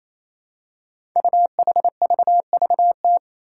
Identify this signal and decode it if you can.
Morse code, heard as a keyed tone, about 27 wpm, 700 Hz. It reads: UHVVT
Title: Five Challenging Letter-Number Combinations 28wpm